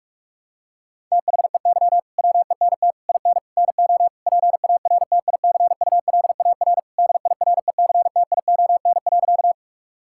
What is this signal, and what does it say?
Morse code, heard as a keyed tone, about 41 wpm, 700 Hz. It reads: THEY WENT IN NO PARTICULAR DIRECTION.
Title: Sentences from Top 1000 Words - Review Part 5 - 45wpm